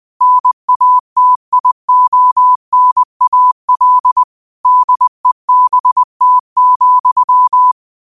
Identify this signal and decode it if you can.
Morse code, heard as a keyed tone, about 20 wpm, 1000 Hz. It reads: NATIONAL DEBT,
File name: qotdfiles / 20wpm